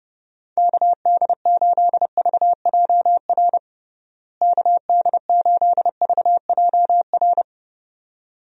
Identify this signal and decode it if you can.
Morse code, heard as a keyed tone, about 30 wpm, 700 Hz. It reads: KD8VJR KD8VJR
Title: USA Call Signs - 6 Characters - Repeated 2x - 30wpm